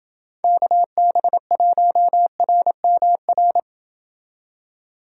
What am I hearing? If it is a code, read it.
Morse code, heard as a keyed tone, about 29 wpm, 700 Hz. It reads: KB1RMR